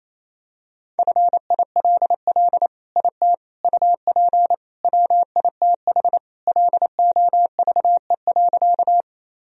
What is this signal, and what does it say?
Morse code, heard as a keyed tone, about 28 wpm, 700 Hz. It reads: FILL IT UP WITH LOVE.
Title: Sentences from Top 1000 Words - Review Part 3 - 28wpm